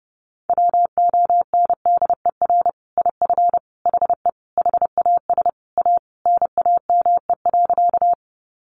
Morse code, heard as a keyed tone, about 30 wpm, 700 Hz. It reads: WONDER IF HE HAS A NAME.
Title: Sentences from Top 500 Words 30wpm